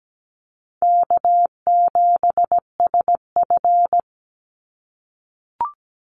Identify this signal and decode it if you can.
Morse code, heard as a keyed tone, about 17 wpm, 700 Hz. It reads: K7SF
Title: Easy Call Signs - 17wpm